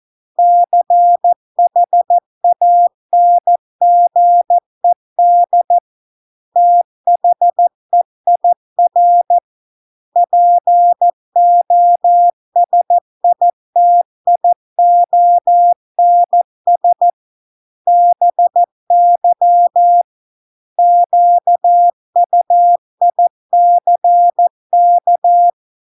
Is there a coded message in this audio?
Yes: Morse code from 0.4 to 25.5 s, heard as a keyed tone, about 14 wpm, 700 Hz. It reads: CHANGED THEIR POSITIONS BY QUICK